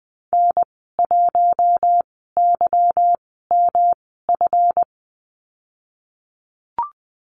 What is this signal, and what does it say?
Morse code, heard as a keyed tone, about 20 wpm, spaced as for 15 wpm, 700 Hz. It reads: N1YMF
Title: 147 Lesson 24g - Letter y - Call Signs 20wpm